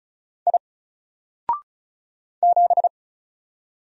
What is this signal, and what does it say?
Morse code, heard as a keyed tone, about 34 wpm, 700 Hz. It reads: I 7
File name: Single-Letter-Number-35wpm